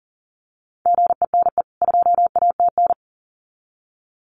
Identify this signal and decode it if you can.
Morse code, heard as a keyed tone, about 40 wpm, 700 Hz. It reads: GENE 2ATN
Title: CWT Exchange 40wpm